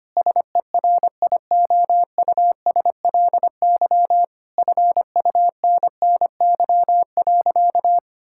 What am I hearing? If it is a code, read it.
Morse code, heard as a keyed tone, about 25 wpm, 700 Hz. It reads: SERIOUSLY FUNNY.
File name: practice / 25wpm / Difficult Sentences with Hints - 25wpm